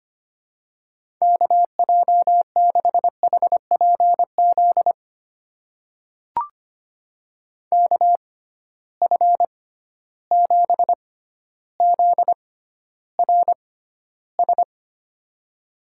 Morse code, heard as a keyed tone, about 25 wpm, spaced as for 13 wpm, 700 Hz. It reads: KJ6HPZ KF7ZRS
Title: USA Call Signs - 6 Characters - 6x Character Spacing - 25wpm